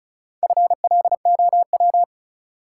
Morse code, heard as a keyed tone, about 36 wpm, 700 Hz. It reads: FLOW
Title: Sentences from Top 1200 Words - New Word Drill Alt - 35wpm